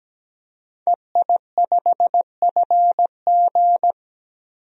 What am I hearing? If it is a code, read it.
Morse code, heard as a keyed tone, about 17 wpm, 700 Hz. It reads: EI5FG
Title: DXCC-100 Call Signs - 5 Characters - 17wpm